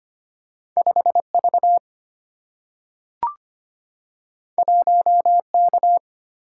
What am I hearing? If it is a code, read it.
Morse code, heard as a keyed tone, about 25 wpm, 700 Hz. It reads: 5 V 1 K
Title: Single-Letter Single-Number 25wpm